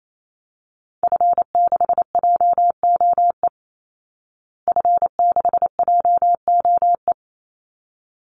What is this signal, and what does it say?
Morse code, heard as a keyed tone, about 28 wpm, 700 Hz. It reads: F6JOE F6JOE
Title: Call Signs Repeated 3x - 28wpm